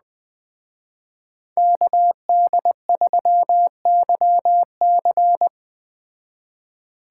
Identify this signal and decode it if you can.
Morse code, heard as a keyed tone, about 20 wpm, 700 Hz. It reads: KD3YC